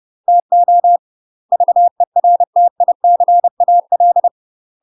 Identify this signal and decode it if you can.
Morse code, heard as a keyed tone, about 30 wpm, 700 Hz. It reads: TO VERTICAL
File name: UK MASTER QSOs TRAINING / 30WPM / UK MASTER QSOs 0023